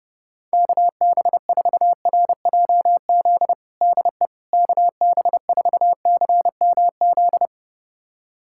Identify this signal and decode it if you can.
Morse code, heard as a keyed tone, about 32 wpm, 700 Hz. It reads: KB4RJZ DE KB4CMZ
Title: USA Callsign DE Callsign - 6 Characters - 30wpm